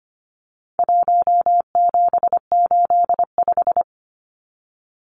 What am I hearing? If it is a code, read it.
Morse code, heard as a keyed tone, about 25 wpm, 700 Hz. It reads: 1785
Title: Numbers - 4-Digit 25wpm